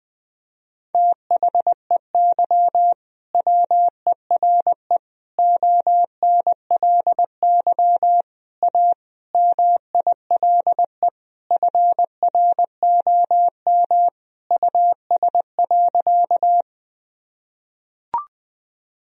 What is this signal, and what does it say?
Morse code, heard as a keyed tone, about 20 wpm, 700 Hz. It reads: THEY WERE ONLY A MILE FROM US.
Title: Sentences from Top 1000 Words - Review Part 1 - 20wpm